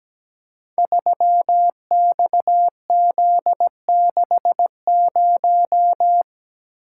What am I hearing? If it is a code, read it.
Morse code, heard as a keyed tone, about 17 wpm, 700 Hz. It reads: 3XZ60
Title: Code Groups - 5 Letter:Numbers - 17wpm